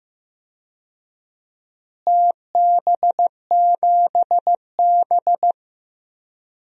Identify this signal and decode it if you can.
Morse code, heard as a keyed tone, about 15 wpm, 700 Hz. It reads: TB7B
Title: DXCC-100 Call Signs - 4 Characters - Repeated 2x - 15wpm